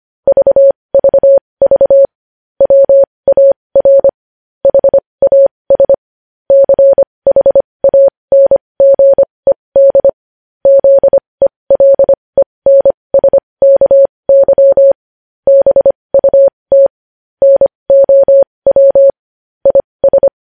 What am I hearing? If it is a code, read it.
Morse code, heard as a keyed tone, about 25 wpm, spaced as for 20 wpm, 550 Hz. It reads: VVV WAR HAS CHANGED ZELENSKY BUT NOW IS
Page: news headlines in morse code at 20 wpm